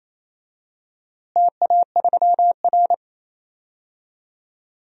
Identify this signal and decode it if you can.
Morse code, heard as a keyed tone, about 28 wpm, 700 Hz. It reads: TA3R
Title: DXCC-100 Call Signs - 4 Characters - Repeated 2x - 28wpm